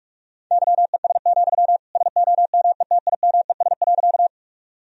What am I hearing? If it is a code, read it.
Morse code, heard as a keyed tone, about 45 wpm, 700 Hz. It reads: YES, SOMETIMES.
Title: Sentences from Top 1000 Words - Review Part 3 - 45wpm